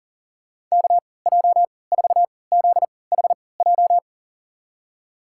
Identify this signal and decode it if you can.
Morse code, heard as a keyed tone, about 40 wpm, spaced as for 23 wpm, 700 Hz. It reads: KJ4ZHJ
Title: USA Call Signs - 6 Characters - 3x Character Spacing - 40wpm